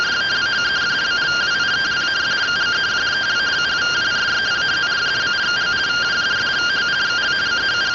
Two-tone teleprinter signal, about 170 Hz shift.